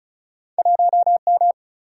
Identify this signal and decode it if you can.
Morse code, heard as a keyed tone, about 36 wpm, 700 Hz. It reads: 1 M
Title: Single-Letter Single-Number 35wpm